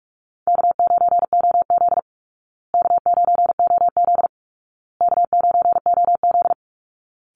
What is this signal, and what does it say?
Morse code, heard as a keyed tone, about 44 wpm, 700 Hz. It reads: K9OZ K9OZ K9OZ
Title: Easy Call Signs Repeated 3x - 45wpm